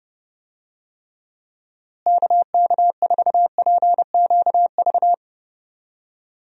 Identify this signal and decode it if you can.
Morse code, heard as a keyed tone, about 32 wpm, 700 Hz. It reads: KK4PQV
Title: USA Call Signs - 6 Characters - Repeated 2x - 30wpm